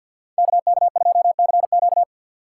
Morse code, heard as a keyed tone, about 47 wpm, 700 Hz. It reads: KK2CQ
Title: USA Call Signs - 5 Characters - 2x Character Spacing - 50wpm